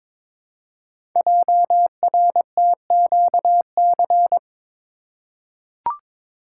Morse code, heard as a keyed tone, about 22 wpm, 700 Hz. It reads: JRTQC
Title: Code Groups - 5 Letters - 22wpm